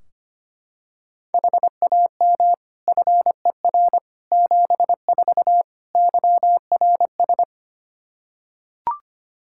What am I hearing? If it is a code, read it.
Morse code, heard as a keyed tone, about 25 wpm, 700 Hz. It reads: HAM FER 74 YRS
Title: Practice QSO - Exchange 3 - More Info - 25wpm